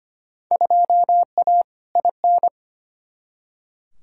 Morse code, heard as a keyed tone, about 25 wpm, 700 Hz. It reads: 2A IN
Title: Field Day Exchanges - 25wpm